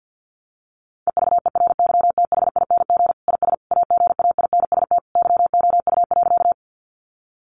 Morse code, heard as a keyed tone, about 50 wpm, 700 Hz. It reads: EVERYTHING IS AGAINST YOU.
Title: Sentences from Top 700 Words 50wpm